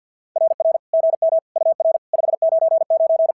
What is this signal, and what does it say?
Morse code, heard as a keyed tone, about 50 wpm, 650 Hz. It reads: RR GM UR 599
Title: SOTA QSO Practice 50wpm